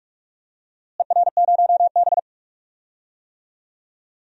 Morse code, heard as a keyed tone, about 46 wpm, 700 Hz. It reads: ER0B
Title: DXCC-100 Call Signs - 4 Characters - Repeated 3x - 45wpm